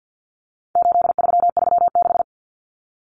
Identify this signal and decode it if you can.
Morse code, heard as a keyed tone, about 48 wpm, 700 Hz. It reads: 8336
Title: Common Cut Numbers - 4-Digit 50wpm